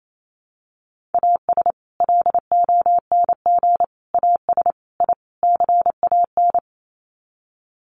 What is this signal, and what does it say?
Morse code, heard as a keyed tone, about 28 wpm, 700 Hz. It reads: AS LONG AS I CAN
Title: Sets of 5 Words Encore2 28wpm